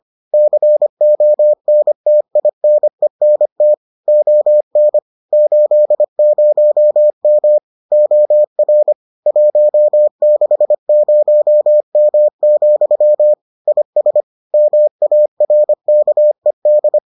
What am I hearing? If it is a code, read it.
Morse code, heard as a keyed tone, about 25 wpm, 600 Hz. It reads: CONTINENT ON 80M OR 160M, IS MARKED